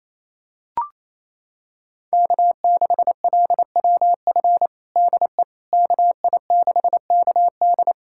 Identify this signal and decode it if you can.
Morse code, heard as a keyed tone, about 28 wpm, 700 Hz. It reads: K6LWF DE KI6KD
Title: USA Callsign DE Callsign - 5 Characters - 28wpm